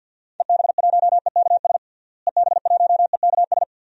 Morse code, heard as a keyed tone, about 51 wpm, 700 Hz. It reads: EB1EKS EB1EKS
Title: DXCC-100 Call Signs - 6 Characters - Repeated 2x - 50wpm